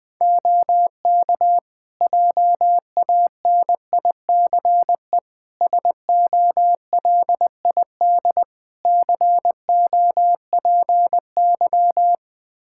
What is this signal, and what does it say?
Morse code, heard as a keyed tone, about 21 wpm, 700 Hz. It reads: OK JANICE SOLID COPY
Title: Practice QSO - Exchange 3 - More Info - 20wpm